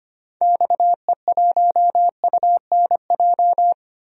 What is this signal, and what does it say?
Morse code, heard as a keyed tone, about 25 wpm, 700 Hz. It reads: XE1UNJ
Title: DXCC-100 Call Signs - 6 Characters - 25wpm